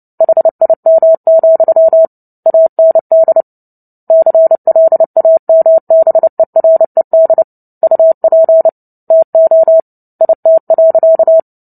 Morse code, heard as a keyed tone, about 30 wpm, 650 Hz. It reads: HIM, AND CLAMBERED UP TO IT.
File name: Book Two / 03-Chapter 3 - 29 WPM